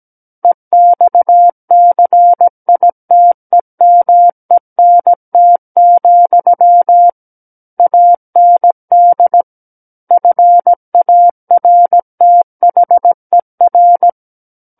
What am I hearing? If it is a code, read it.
Morse code, heard as a keyed tone, about 17 wpm, 700 Hz. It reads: EXCITEMENT, AND FARTHER
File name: Book One / 08-Chapter 8 - 17 WPM